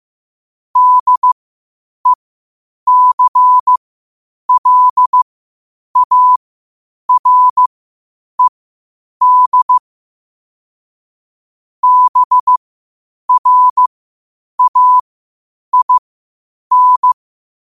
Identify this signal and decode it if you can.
Morse code, heard as a keyed tone, about 15 wpm, spaced as for 8 wpm, 1000 Hz. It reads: DECLARED BRAIN
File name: qotdfiles / 8wpm